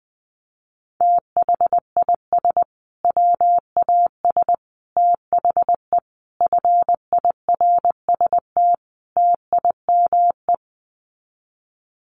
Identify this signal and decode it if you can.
Morse code, heard as a keyed tone, about 20 wpm, 700 Hz. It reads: THIS WAS THE FIRST TIME
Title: Sets of 5 Words Encore 20wpm